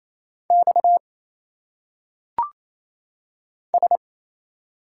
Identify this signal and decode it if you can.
Morse code, heard as a keyed tone, about 28 wpm, 700 Hz. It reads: X S
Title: Single Letters - Phonetic Pronunciation 28wpm